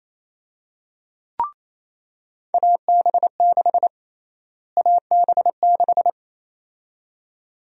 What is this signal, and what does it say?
Morse code, heard as a keyed tone, about 28 wpm, 700 Hz. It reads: AB6 AB6
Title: USA Call Sign Prefixes - Repeated 2x - 28wpm